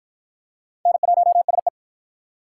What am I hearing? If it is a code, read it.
Morse code, heard as a keyed tone, about 52 wpm, 700 Hz. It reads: N1SE